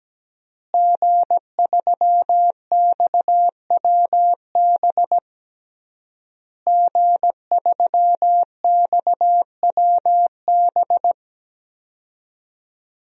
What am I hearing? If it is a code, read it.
Morse code, heard as a keyed tone, about 17 wpm, 700 Hz. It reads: G3XWB G3XWB
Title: DXCC-100 Call Signs - 5 Characters - Repeated 3x - 17wpm